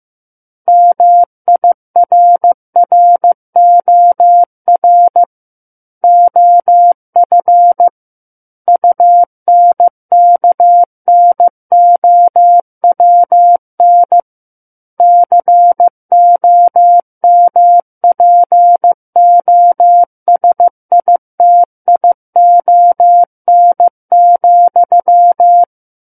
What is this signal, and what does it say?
Morse code, heard as a keyed tone, about 15 wpm, 700 Hz. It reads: MIRROR OF UNKNOWN COMPOSITION,
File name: Chapter 6 - 15 WPM